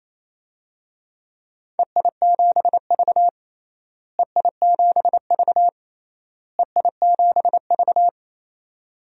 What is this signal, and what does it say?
Morse code, heard as a keyed tone, about 27 wpm, 700 Hz. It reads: EI7V EI7V EI7V
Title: DXCC-100 Call Signs - 4 Characters - Repeated 3x - 28wpm